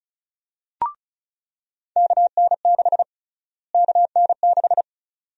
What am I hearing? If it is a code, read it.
Morse code, heard as a keyed tone, about 34 wpm, 700 Hz. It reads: KN6 KN6
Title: USA Call Sign Prefixes - Repeated 2x - 35wpm